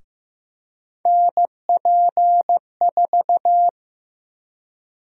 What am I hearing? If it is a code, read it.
Morse code, heard as a keyed tone, about 15 wpm, 700 Hz. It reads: NP4
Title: USA Call Sign Prefixes - 15wpm